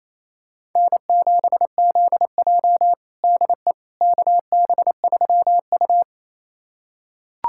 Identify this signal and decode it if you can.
Morse code, heard as a keyed tone, about 28 wpm, 700 Hz. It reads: N7ZJ DE KB3U
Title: USA Callsign DE Callsign - 4 Characters - 28wpm